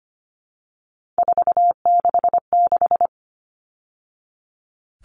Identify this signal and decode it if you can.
Morse code, heard as a keyed tone, about 25 wpm, 700 Hz. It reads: 466
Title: All Cut Numbers - 3-Digit 25wpm